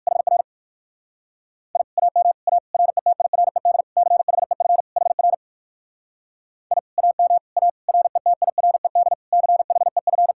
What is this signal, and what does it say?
Morse code, heard as a keyed tone, about 65 wpm, spaced as for 57 wpm, 700 Hz. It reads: HR I AM A RETIRED CHEF HR I AM A RETIRED CHEF